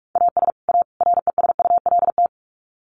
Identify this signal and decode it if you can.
Morse code, heard as a keyed tone, about 44 wpm, 700 Hz. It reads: AS A RESULT
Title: Sets of 3 Words 45wpm